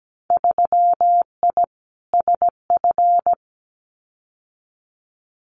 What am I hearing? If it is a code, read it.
Morse code, heard as a keyed tone, about 17 wpm, 700 Hz. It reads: 3I SF
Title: Winter Field Day Exchanges 17wpm